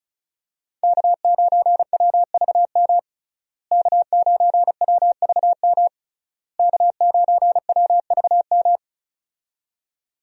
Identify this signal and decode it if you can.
Morse code, heard as a keyed tone, about 34 wpm, 700 Hz. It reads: K9WVM K9WVM K9WVM
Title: USA Call Signs - 5 Characters - Repeated 3x - 35wpm